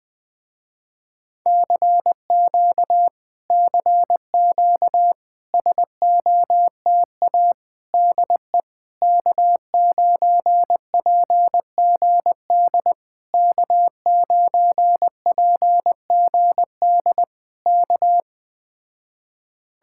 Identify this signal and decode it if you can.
Morse code, heard as a keyed tone, about 20 wpm, 700 Hz. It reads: CQ CQ SOTA DE K9PGD K9PGD K
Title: Speed-Racing Common QSO Phrases - 17wpm